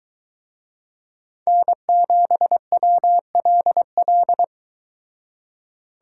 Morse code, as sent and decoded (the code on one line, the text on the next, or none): -. --... .-- .-.. .-..
N7WLL